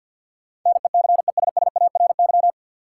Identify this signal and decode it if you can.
Morse code, heard as a keyed tone, about 50 wpm, 700 Hz. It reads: NECESSARY
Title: Sentences from Top 800 Words - New Word Drill - 50wpm